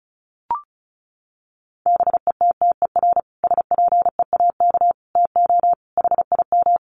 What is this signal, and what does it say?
Morse code, heard as a keyed tone, about 34 wpm, 700 Hz. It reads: BETTER SPEAK TO HIM
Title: Sentences from Top 1100 Words 35wpm